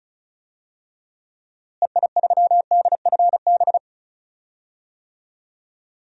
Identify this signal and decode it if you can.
Morse code, heard as a keyed tone, about 36 wpm, 700 Hz. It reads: EI3DFB
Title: DXCC-100 Call Signs - 6 Characters - Repeated 3x - 35wpm